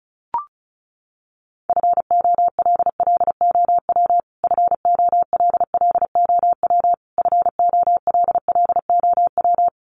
Morse code, heard as a keyed tone, about 34 wpm, 700 Hz. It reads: FOLLOW FOLLOW FOLLOW
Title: Top 200 Words Repeated 35wpm